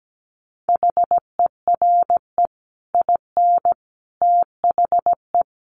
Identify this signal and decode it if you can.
Morse code, heard as a keyed tone, about 17 wpm, 700 Hz. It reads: HERE IN THE
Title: Sets of 5 Words 17wpm